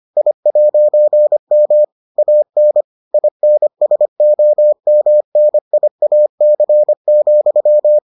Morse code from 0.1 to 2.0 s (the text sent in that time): I'M